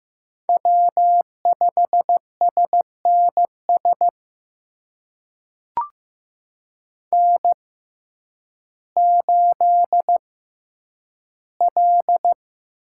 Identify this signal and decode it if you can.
Morse code, heard as a keyed tone, about 15 wpm, spaced as for 9 wpm, 700 Hz. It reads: W5SNS N8L
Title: USA Call Signs - 5 Characters - 6x Character Spacing - 15wpm